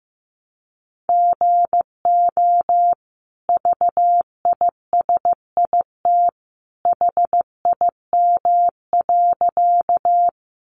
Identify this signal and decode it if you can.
Morse code, heard as a keyed tone, about 15 wpm, 700 Hz. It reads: GO VISIT HIM.
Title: Sentences from Top 900 Words 15wpm